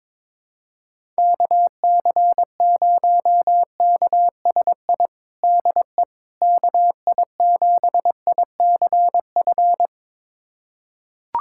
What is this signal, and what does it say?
Morse code, heard as a keyed tone, about 22 wpm, 700 Hz. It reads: KC0KSI DE KI7ICF
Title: USA Callsign DE Callsign - 6 Characters - 22wpm